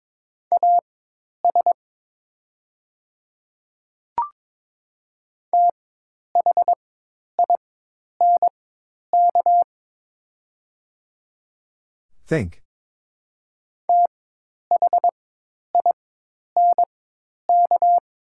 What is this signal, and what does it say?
Morse code, heard as a keyed tone, about 22 wpm, spaced as for 10 wpm, 700 Hz. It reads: AS THINK THINK
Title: 4x Character Spacing - Top 300 Words - 22wpm